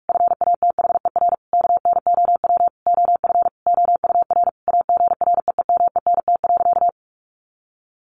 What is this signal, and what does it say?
Morse code, heard as a keyed tone, about 44 wpm, 700 Hz. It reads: FATHER KNOW OF OUR AGREEMENT.
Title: Sentences from Top 1500 Words 45wpm